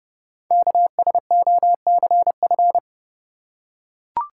Morse code, heard as a keyed tone, about 30 wpm, 700 Hz. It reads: KSOCF
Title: Code Groups - 5 Letter:Numbers - 30wpm